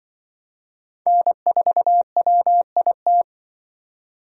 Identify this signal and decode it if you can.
Morse code, heard as a keyed tone, about 23 wpm, 700 Hz. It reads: N4WIT
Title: Speed Racing - USA Call Signs 20wpm